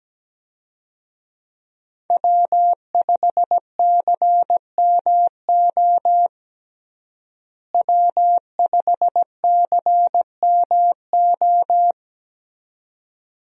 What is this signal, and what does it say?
Morse code, heard as a keyed tone, about 17 wpm, 700 Hz. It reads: W5CMO W5CMO
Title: USA Call Signs - 5 Characters - Repeated 2x - 17wpm